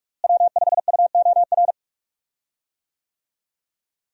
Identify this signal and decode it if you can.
Morse code, heard as a keyed tone, about 46 wpm, 700 Hz. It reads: W5UOR
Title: USA Call Signs - 5 Characters - Repeated 3x - 45wpm